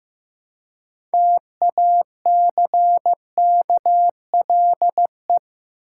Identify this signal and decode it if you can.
Morse code, heard as a keyed tone, about 15 wpm, 700 Hz. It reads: TACKLE
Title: Six Letter Words - 15wpm